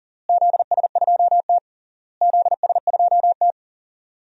Morse code, heard as a keyed tone, about 40 wpm, 700 Hz. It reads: ZS2T ZS2T
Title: DXCC-100 Call Signs - 4 Characters - Repeated 3x - 40wpm